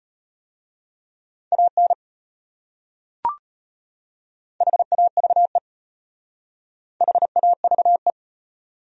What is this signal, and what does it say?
Morse code, heard as a keyed tone, about 33 wpm, 700 Hz. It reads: AN HAVE HAVE